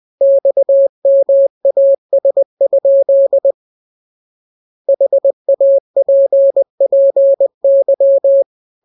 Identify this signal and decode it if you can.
Morse code, heard as a keyed tone, about 20 wpm, 550 Hz. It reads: XMAS? HAPPY